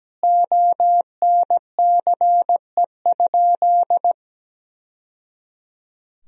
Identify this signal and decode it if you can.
Morse code, heard as a keyed tone, about 17 wpm, 700 Hz. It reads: ONCE?
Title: Sentences from Top 1000 Words - Review Part 1 - 17wpm